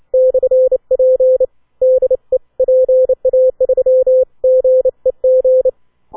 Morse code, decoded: /P DE PA3GEG